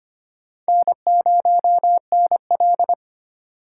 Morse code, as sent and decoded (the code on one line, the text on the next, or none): -. ----- -. .-..
N0NL